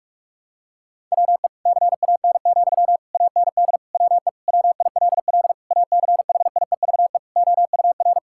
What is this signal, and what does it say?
Morse code, heard as a keyed tone, about 41 wpm, 700 Hz. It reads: WE CAN, AND WE WILL ACHIEVE OUR
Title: Sentences from Top 1000 Words - Review Part 6 - 45wpm